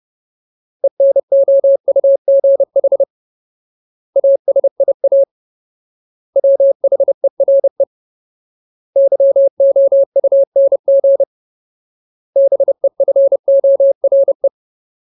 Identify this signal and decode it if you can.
Morse code, heard as a keyed tone, about 30 wpm, 550 Hz. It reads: ENOUGH ASIA WHERE YOUNG BEFORE
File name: Ultimate Word List / 30WPM Ultimate